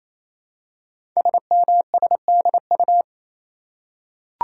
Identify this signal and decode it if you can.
Morse code, heard as a keyed tone, about 28 wpm, 700 Hz. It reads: SMSDU
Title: Code Groups - 5 Letters - 28wpm